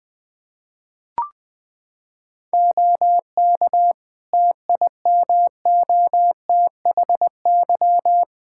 Morse code, heard as a keyed tone, about 20 wpm, 700 Hz. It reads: OK TIMOTHY
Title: K1USN SST QSO Practice 20wpm